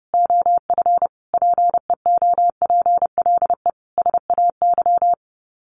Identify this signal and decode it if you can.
Morse code, heard as a keyed tone, about 30 wpm, 700 Hz. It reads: OF PEOPLE SAY
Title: Sets of 5 Words Encore2 30wpm